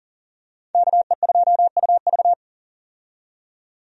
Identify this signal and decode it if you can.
Morse code, heard as a keyed tone, about 40 wpm, 700 Hz. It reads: KE2UV
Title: USA Call Signs - 5 Characters - 40wpm